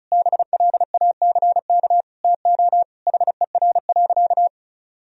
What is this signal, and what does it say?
Morse code, heard as a keyed tone, about 36 wpm, 700 Hz. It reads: BLACK TO HER.
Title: Sentences from Top 1000 Words - Review Part 2 - 35wpm